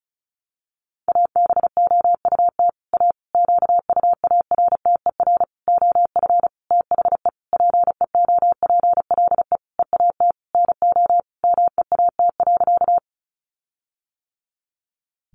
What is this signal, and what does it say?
Morse code, heard as a keyed tone, about 34 wpm, 700 Hz. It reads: ABOUT A QUARTER OF THE PEOPLE EAT NO MEAT.
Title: Sentences from Top 1300 Words 35wpm